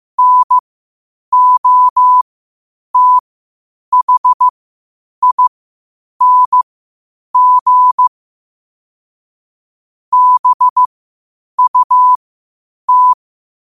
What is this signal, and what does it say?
Morse code, heard as a keyed tone, about 15 wpm, spaced as for 8 wpm, 1000 Hz. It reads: NOTHING BUT
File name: qotdfiles / 8wpm